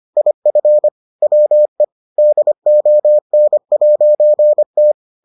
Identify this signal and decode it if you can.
Morse code, heard as a keyed tone, about 25 wpm, 600 Hz. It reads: IF WE DON'T